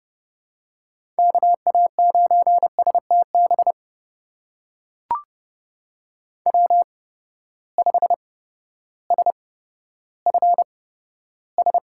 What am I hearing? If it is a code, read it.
Morse code, heard as a keyed tone, about 30 wpm, spaced as for 14 wpm, 700 Hz. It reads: KA9STB W5SFS
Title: USA Call Signs - 8x Character Spacing - 30wpm